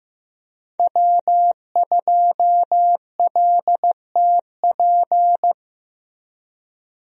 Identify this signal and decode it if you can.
Morse code, heard as a keyed tone, about 15 wpm, 700 Hz. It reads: W2LTP
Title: USA Call Signs - 5 Characters - Repeated 3x - 15wpm